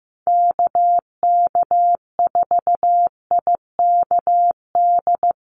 Morse code, heard as a keyed tone, about 15 wpm, 700 Hz. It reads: KK4IKD
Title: USA Call Signs Repeated 2x 15wpm